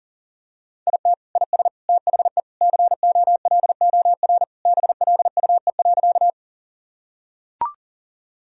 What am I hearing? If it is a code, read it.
Morse code, heard as a keyed tone, about 40 wpm, 700 Hz. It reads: IT IS THE COLOR BLUE.
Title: Sentences from Top 1000 Words - Review Part 3 - 40wpm